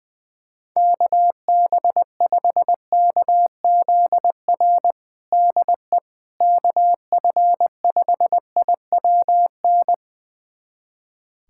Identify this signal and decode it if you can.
Morse code, heard as a keyed tone, about 21 wpm, 700 Hz. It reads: KB5KZR DE KF5IWN
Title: USA Callsign DE Callsign - 6 Characters - 20wpm